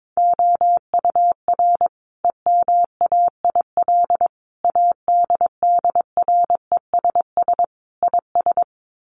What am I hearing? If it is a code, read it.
Morse code, heard as a keyed tone, about 22 wpm, 700 Hz. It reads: OUR EMAIL ADDRESS IS
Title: Sets of 5 Words Encore2 22wpm